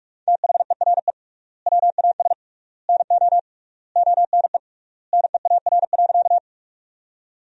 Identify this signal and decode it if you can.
Morse code, heard as a keyed tone, about 42 wpm, 700 Hz. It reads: THERE WAS NO ONE NEAR.
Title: 3x Word-Spacing - Sentences from Top 200 Words - 45wpm